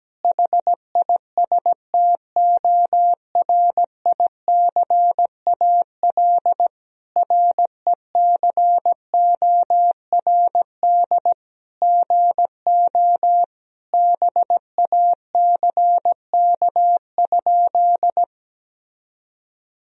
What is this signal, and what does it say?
Morse code, heard as a keyed tone, about 17 wpm, 700 Hz. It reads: HISTORICAL RECORD GO BACK?
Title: Sentences from Top 1000 Words - Review Part 4 - 17wpm